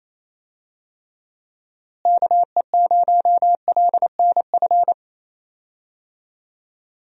Morse code, heard as a keyed tone, about 28 wpm, 700 Hz. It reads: KE0LNF
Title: USA Call Signs - 6 Characters - Repeated 3x - 28wpm